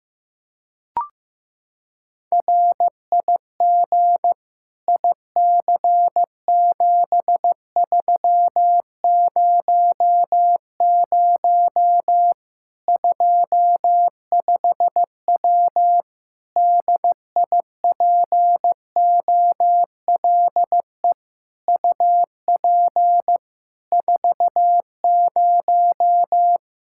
Morse code, heard as a keyed tone, about 15 wpm, 700 Hz. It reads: RIG IC7300 25W DIPOLE UP 40
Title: K1USN SST QSO Practice 15wpm